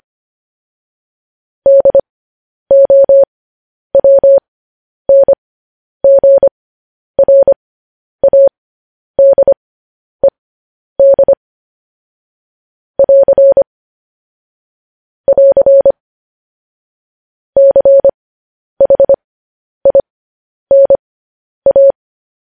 Morse code, heard as a keyed tone, about 25 wpm, spaced as for 10 wpm, 550 Hz. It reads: DOWNGRADED <AR> <AR> CHINA
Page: news headlines in morse code at 10 wpm